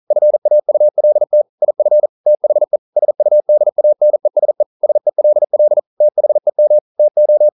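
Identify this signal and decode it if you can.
Morse code, heard as a keyed tone, about 42 wpm, 600 Hz. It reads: FAULT IF THE SUDANESE SELL THEM TO
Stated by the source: Morse code practice at 41 wpm, from CW med Gnister.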